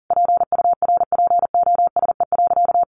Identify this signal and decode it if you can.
Morse code, heard as a keyed tone, about 40 wpm, 700 Hz. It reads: PURPOSE.
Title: Sentences from Top 800 Words 40wpm